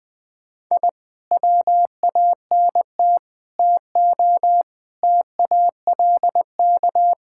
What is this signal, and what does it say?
Morse code, heard as a keyed tone, about 20 wpm, 700 Hz. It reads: I WANT TO TALK
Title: Sentences with 4-Word Phrases - 20wpm